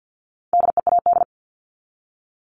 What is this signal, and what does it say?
Morse code, heard as a keyed tone, about 50 wpm, 700 Hz. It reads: DEAD
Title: Top 1251-1500 Words 50wpm